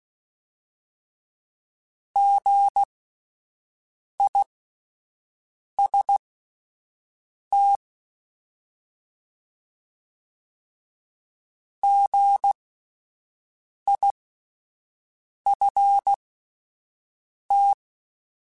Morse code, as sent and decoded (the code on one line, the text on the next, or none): --. .. ... - / --. .. ..-. -
GIST GIFT